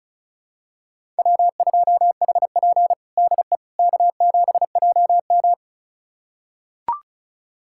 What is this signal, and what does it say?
Morse code, heard as a keyed tone, about 34 wpm, 700 Hz. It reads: W2HP DE K7JM
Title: USA Callsign DE Callsign - 4 Characters - 35wpm